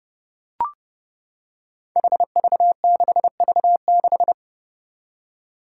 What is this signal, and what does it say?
Morse code, heard as a keyed tone, about 30 wpm, 700 Hz. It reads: HV6V6
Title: Five Challenging Letter-Number Combinations 30wpm